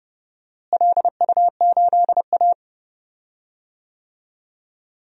Morse code, heard as a keyed tone, about 30 wpm, 700 Hz. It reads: LU8A
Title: DXCC-100 Call Signs - 4 Characters - Repeated 3x - 30wpm